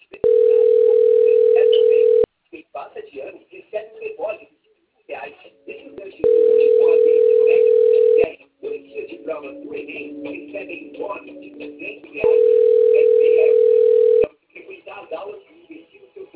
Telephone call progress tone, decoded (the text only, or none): ringback tone